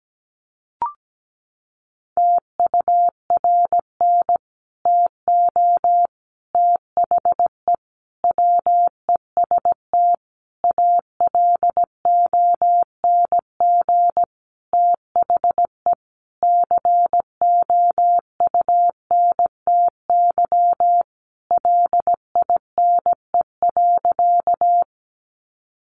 Morse code, heard as a keyed tone, about 17 wpm, 700 Hz. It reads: TURN TO THE WEST ALONG THE COUNTY LINE.
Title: Sentences from Top 1200 Words 17wpm